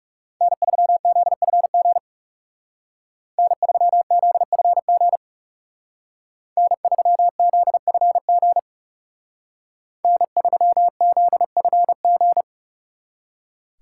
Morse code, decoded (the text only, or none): N3ZFG N3ZFG N3ZFG N3ZFG